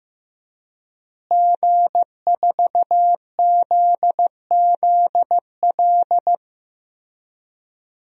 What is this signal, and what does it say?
Morse code, heard as a keyed tone, about 15 wpm, 700 Hz. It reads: G4ZZL
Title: DXCC-100 Call Signs - 5 Characters - Repeated 3x - 15wpm